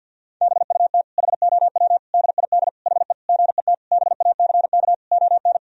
Morse code, heard as a keyed tone, about 50 wpm, 700 Hz. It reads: BUT HOW DID HE GET BACK ON
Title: Sentences from Top 1000 Words - Review Part 4 - 50wpm